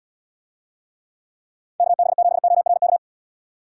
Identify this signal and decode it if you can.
Morse code, heard as a keyed tone, about 74 wpm, 690 Hz. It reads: KB7ORK